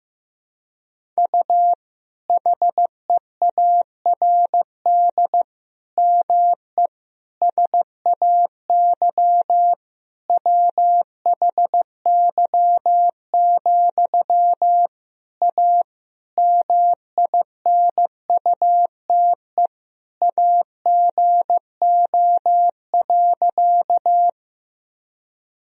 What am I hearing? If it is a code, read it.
Morse code, heard as a keyed tone, about 15 wpm, 700 Hz. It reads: U HEARD ME SAY WHY, A MINUTE AGO.
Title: Sentences from Top 500 Words and Abbreviations 15wpm